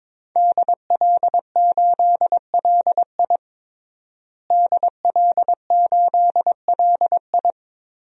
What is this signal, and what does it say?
Morse code, heard as a keyed tone, about 22 wpm, 700 Hz. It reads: DL8LI DL8LI
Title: DXCC-100 Call Signs - 5 Characters - Repeated 3x - 22wpm